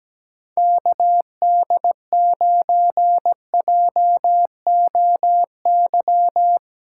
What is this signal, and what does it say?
Morse code, heard as a keyed tone, about 17 wpm, 700 Hz. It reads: KD9JOY
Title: USA Callsign DE Callsign - 6 Characters - 17wpm